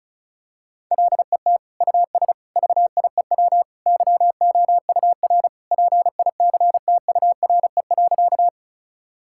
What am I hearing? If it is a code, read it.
Morse code, heard as a keyed tone, about 34 wpm, 690 Hz. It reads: LET US VIEW YOUR PICTURE.
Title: Sentences from Top 1000 Words - Review Part 6 - 35wpm